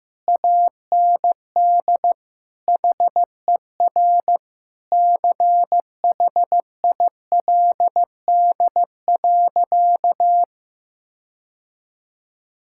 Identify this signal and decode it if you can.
Morse code, heard as a keyed tone, about 15 wpm, 700 Hz. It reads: AND HER CHILD.
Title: Sentences from Top 1600 Words 15wpm